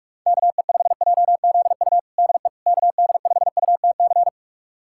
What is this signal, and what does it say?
Morse code, heard as a keyed tone, about 42 wpm, 700 Hz. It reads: KE5JZU DE KD5UTC
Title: USA Callsign DE Callsign 45wpm